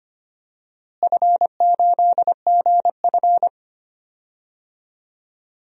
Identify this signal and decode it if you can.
Morse code, heard as a keyed tone, about 25 wpm, 700 Hz. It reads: F8GF
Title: DXCC-100 Call Signs - 4 Characters - Repeated 3x - 25wpm